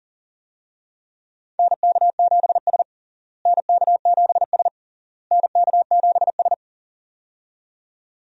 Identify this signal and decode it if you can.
Morse code, heard as a keyed tone, about 40 wpm, 700 Hz. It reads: NK7S NK7S NK7S
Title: USA Call Signs - 4 Characters - Repeated 3x - 40wpm